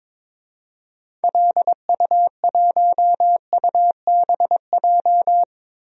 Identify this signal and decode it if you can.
Morse code, heard as a keyed tone, about 22 wpm, 700 Hz. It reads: LU1UBJ
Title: DXCC-100 Call Signs - 6 Characters - 22wpm